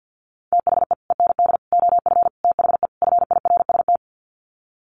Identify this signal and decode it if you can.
Morse code, heard as a keyed tone, about 49 wpm, 700 Hz. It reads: THE END OF THE FIRST
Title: Sets of 5 Words Encore 50wpm